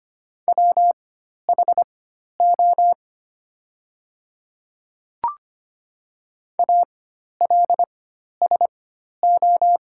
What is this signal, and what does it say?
Morse code, heard as a keyed tone, about 25 wpm, spaced as for 12 wpm, 700 Hz. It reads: WHO ALSO
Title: 4x Character Spacing - Top 300 Words - 25wpm